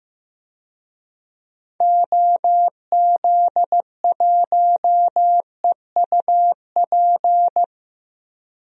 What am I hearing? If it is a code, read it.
Morse code, heard as a keyed tone, about 15 wpm, 700 Hz. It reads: OZ1EUP
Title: DXCC-100 Call Signs - 6 Characters - Repeated 2x - 15wpm